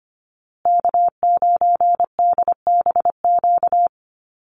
Morse code, heard as a keyed tone, about 25 wpm, 700 Hz. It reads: K9DBQ
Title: USA Call Signs Repeated 3x 25wpm